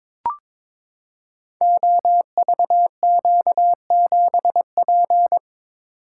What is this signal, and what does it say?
Morse code, heard as a keyed tone, about 22 wpm, 700 Hz. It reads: OVQ7P
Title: Code Groups - 5 Letter:Numbers - 22wpm